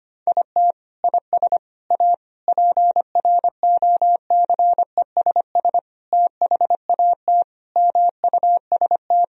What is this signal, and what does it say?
Morse code, heard as a keyed tone, about 25 wpm, 700 Hz. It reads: IT IS A PROCESS THAT MUST
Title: Sentences from Top 1000 Words - Review Part 3 - 25wpm